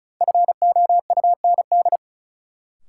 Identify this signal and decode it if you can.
Morse code, heard as a keyed tone, about 35 wpm, 700 Hz. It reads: FOUND
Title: Sentences from Top 300 Words - New Word Drill - 35wpm